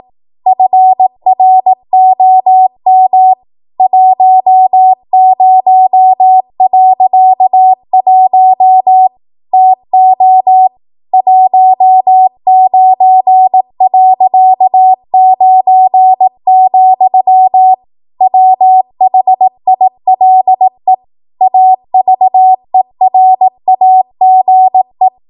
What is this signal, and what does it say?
Morse code, heard as a keyed tone, about 18 wpm, 750 Hz. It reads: FROM 10.1 TO 19.9, WHILE AVERAGE
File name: Archive / 18 WPM / 18WPM